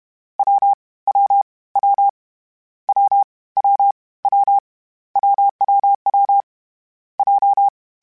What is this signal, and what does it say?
Morse code, heard as a keyed tone, about 31 wpm, 800 Hz. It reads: W W W W W W WWW J
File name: Difficult Combos / Difficult Combos-mp3s / AWJ @30